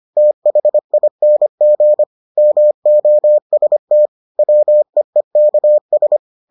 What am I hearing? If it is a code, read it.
Morse code, heard as a keyed tone, about 25 wpm, 600 Hz. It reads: THING MOST WEEKS